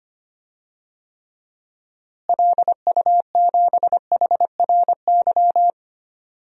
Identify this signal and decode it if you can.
Morse code, heard as a keyed tone, about 25 wpm, 700 Hz. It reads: LU7HRY
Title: DXCC-100 Call Signs - 6 Characters - Repeated 3x - 25wpm